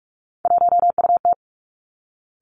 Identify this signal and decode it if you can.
Morse code, heard as a keyed tone, about 46 wpm, 700 Hz. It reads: 1UT
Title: Three Challenging Letter-Number Combinations 45wpm